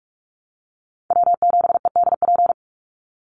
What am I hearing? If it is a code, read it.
Morse code, heard as a keyed tone, about 44 wpm, 700 Hz. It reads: W7EDP
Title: USA Call Signs 45wpm